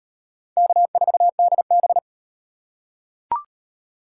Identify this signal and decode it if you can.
Morse code, heard as a keyed tone, about 36 wpm, 700 Hz. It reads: K4DB